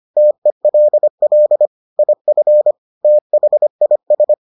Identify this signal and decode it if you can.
Morse code, heard as a keyed tone, about 25 wpm, 600 Hz. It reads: TELL IF THIS